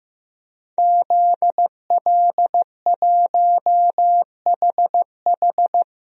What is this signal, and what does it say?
Morse code, heard as a keyed tone, about 15 wpm, 700 Hz. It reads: ZL1HH
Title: DXCC-100 Call Signs - 5 Characters - 15wpm